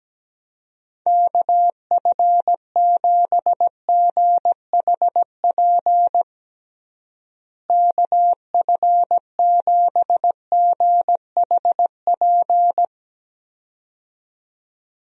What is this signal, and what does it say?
Morse code, heard as a keyed tone, about 17 wpm, 700 Hz. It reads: KF7GHP KF7GHP
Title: USA Call Signs - 6 Characters - Repeated 2x - 17wpm